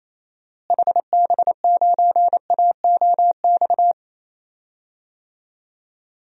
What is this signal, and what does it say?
Morse code, heard as a keyed tone, about 27 wpm, 700 Hz. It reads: HB9AOX
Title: DXCC-100 Call Signs - 6 Characters - Repeated 2x - 28wpm